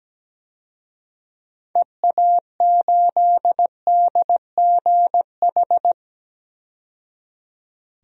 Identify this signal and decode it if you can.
Morse code, heard as a keyed tone, about 17 wpm, 700 Hz. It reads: EA8DGH
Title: DXCC-100 Call Signs - 6 Characters - 17wpm